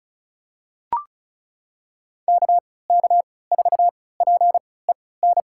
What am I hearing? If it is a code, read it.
Morse code, heard as a keyed tone, about 34 wpm, spaced as for 20 wpm, 690 Hz. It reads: KK4PEN
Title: USA Call Signs - 6 Characters - 3x Character Spacing - 35wpm